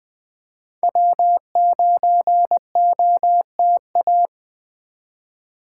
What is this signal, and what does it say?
Morse code, heard as a keyed tone, about 20 wpm, 700 Hz. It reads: W9OTA
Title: USA Call Signs - 2x Character Spacing - 20wpm